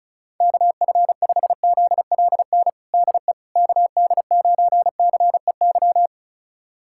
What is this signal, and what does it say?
Morse code, heard as a keyed tone, about 34 wpm, 700 Hz. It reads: KF5ZLN DE KD9CEY
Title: USA Callsign DE Callsign 35wpm